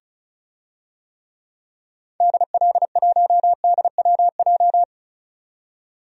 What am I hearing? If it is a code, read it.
Morse code, heard as a keyed tone, about 34 wpm, 700 Hz. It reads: DL1DWJ
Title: DXCC-100 Call Signs - 6 Characters - 35wpm